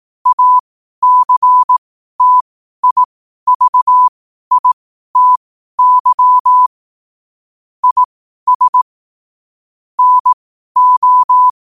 Morse code, heard as a keyed tone, about 18 wpm, spaced as for 12 wpm, 1000 Hz. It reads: ACTIVITY IS NO
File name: qotdfiles / 12wpm